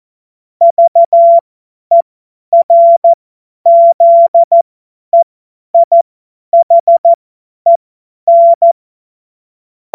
Morse code, heard as a keyed tone, about 14 wpm, spaced as for 10 wpm, 680 Hz. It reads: VERZEIHEN
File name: Morsekurs / Witze / Witz 09. Februar 2021, 50 F70